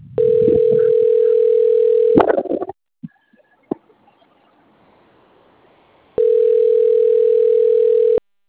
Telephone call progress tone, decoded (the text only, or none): ringback tone